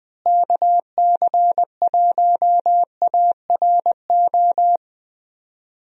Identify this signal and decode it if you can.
Morse code, heard as a keyed tone, about 20 wpm, 700 Hz. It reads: KC1ARO
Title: USA Call Signs - 6 Characters - Repeated 3x - 20wpm